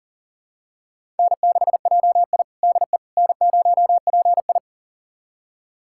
Morse code, heard as a keyed tone, about 37 wpm, 700 Hz. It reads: N6JI DE N0PI
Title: USA Callsign DE Callsign - 4 Characters - 40wpm